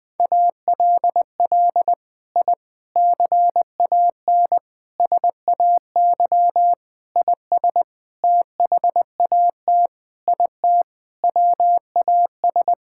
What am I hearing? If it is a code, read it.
Morse code, heard as a keyed tone, about 21 wpm, 700 Hz. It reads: ALL I CAN SAY IS THAT IT WAS
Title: Sentences with 6-Word Phrases - 20wpm